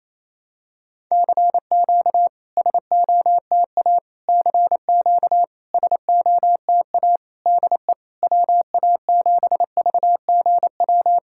Morse code, heard as a keyed tone, about 28 wpm, 690 Hz. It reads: CQ SOTA CQ SOTA DE WA7VGW
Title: SOTA QSO Practice 28wpm